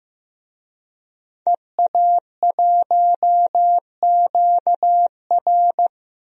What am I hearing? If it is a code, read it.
Morse code, heard as a keyed tone, about 16 wpm, 700 Hz. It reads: EA1QR
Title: DXCC-100 Call Signs - 5 Characters - Repeated 2x - 15wpm